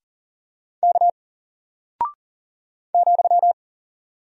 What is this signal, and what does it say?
Morse code, heard as a keyed tone, about 38 wpm, 700 Hz. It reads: K ,
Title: Single Character (Letter:Number:Punctuation:Prosign) 40wpm